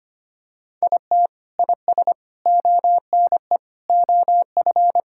Morse code, heard as a keyed tone, about 25 wpm, 700 Hz. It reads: IT IS ONE OF
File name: practice / 25wpm / Sentences with 4-Word Phrases - 25wpm